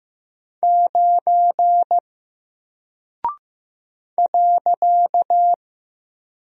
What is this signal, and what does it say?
Morse code, heard as a keyed tone, about 15 wpm, 700 Hz. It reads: 9 .
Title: Single Character (Letter:Number:Punctuation:Prosign) Alt 15wpm